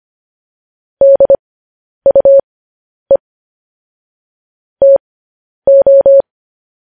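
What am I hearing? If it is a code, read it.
Morse code, heard as a keyed tone, about 25 wpm, spaced as for 10 wpm, 550 Hz. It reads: DUE TO
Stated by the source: Morse code news headlines practice at 10 wpm.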